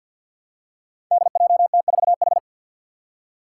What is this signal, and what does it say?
Morse code, heard as a keyed tone, about 53 wpm, 700 Hz. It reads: DJT4H
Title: Five Challenging Letter-Number Combinations 50wpm